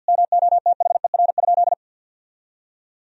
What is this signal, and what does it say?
Morse code, heard as a keyed tone, about 50 wpm, 700 Hz. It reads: MOTHER?
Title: Sentences from Top 1000 Words - Review Part 3 - 50wpm